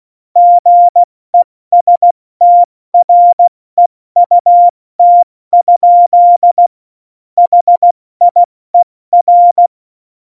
Morse code, heard as a keyed tone, about 16 wpm, spaced as for 14 wpm, 700 Hz. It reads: GESTREUT? HIER
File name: Morsekurs / Witze / Witz 24.Juli 2018, 70 F80